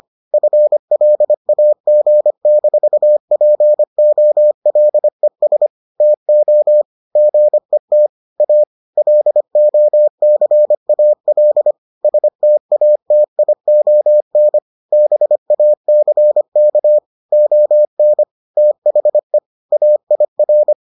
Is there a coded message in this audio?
Yes: Morse code from 0.3 to 20.7 s, heard as a keyed tone, about 25 wpm, 600 Hz. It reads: FLAG-POLES TO GET A LOCAL STATION BACK ON THE AIR